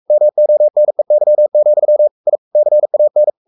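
Morse code, heard as a keyed tone, about 44 wpm, 600 Hz. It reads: MONEY, I CAN
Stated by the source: Morse code at 43 wpm sending Ken Follett's novel Never in English, from CW med Gnister.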